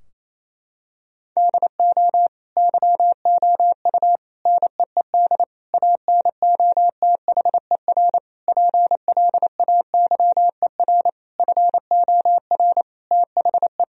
Morse code, heard as a keyed tone, about 28 wpm, 700 Hz. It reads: DO YOU NEED ANOTHER PLAYER FOR THE
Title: Sentences from Top 1000 Words - Review Part 4 - 28wpm